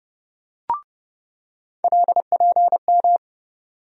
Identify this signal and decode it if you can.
Morse code, heard as a keyed tone, about 30 wpm, 700 Hz. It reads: LPM
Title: USA Call Sign Postfixes - 3 Letters 30wpm